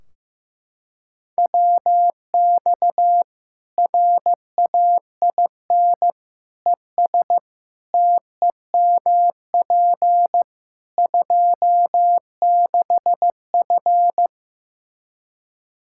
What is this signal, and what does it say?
Morse code, heard as a keyed tone, about 16 wpm, 700 Hz. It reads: WX RAIN ES TEMP 26F
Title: Practice QSO - Exchange 2 - Info - 15wpm